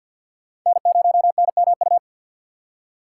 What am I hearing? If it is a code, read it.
Morse code, heard as a keyed tone, about 53 wpm, 700 Hz. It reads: N0NMU
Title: USA Call Signs - 5 Characters - 2x Character Spacing - 50wpm